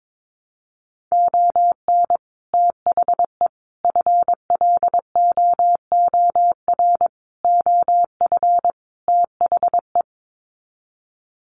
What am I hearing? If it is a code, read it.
Morse code, heard as a keyed tone, about 22 wpm, 700 Hz. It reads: ON THE FLOOR OF THE
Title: Sets of 5 Words 22wpm